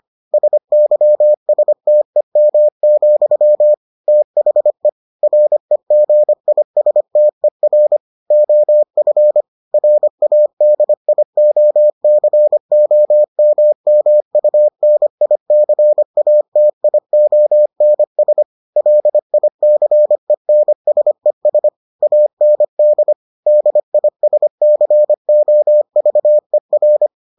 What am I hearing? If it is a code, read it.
Morse code, heard as a keyed tone, about 25 wpm, 600 Hz. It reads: SYSTEM, THE REGISTER OF RADIOCOMMUNICATIONS LICENSES AND DISCOVER